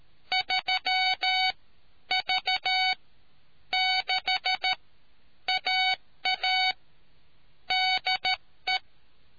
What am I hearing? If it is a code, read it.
Morse code, heard as a keyed tone, about 13 wpm, spaced as for 9 wpm, 750 Hz. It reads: 3V6AA DE